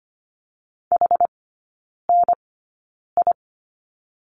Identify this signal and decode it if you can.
Morse code, heard as a keyed tone, about 25 wpm, spaced as for 9 wpm, 700 Hz. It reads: HNI
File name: Lesson 17e - Letter u - Three Characters 25wpm